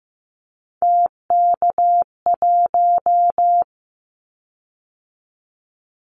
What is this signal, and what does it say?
Morse code, heard as a keyed tone, about 15 wpm, 700 Hz. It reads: TK1
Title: Callsign Prefixes for Top 100 DXCC - 15wpm